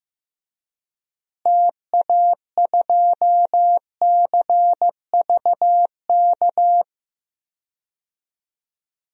Morse code, heard as a keyed tone, about 15 wpm, 700 Hz. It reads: TA2CVK
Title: DXCC-100 Call Signs - 6 Characters - Repeated 3x - 15wpm